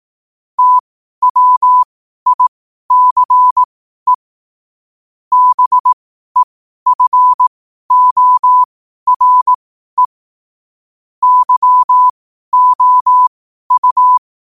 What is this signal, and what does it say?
Morse code, heard as a keyed tone, about 18 wpm, spaced as for 12 wpm, 1000 Hz. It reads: TWICE BEFORE YOU
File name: qotdfiles / 12wpm